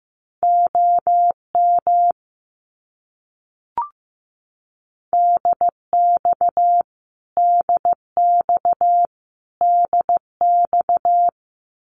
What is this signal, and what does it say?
Morse code, heard as a keyed tone, about 15 wpm, 700 Hz. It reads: OM DX DX DX
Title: Abbreviations Repeated 3x - 15wpm